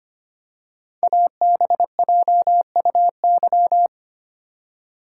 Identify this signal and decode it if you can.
Morse code, heard as a keyed tone, about 25 wpm, 700 Hz. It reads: ABJUY
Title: Code Groups - 5 Letters - 25wpm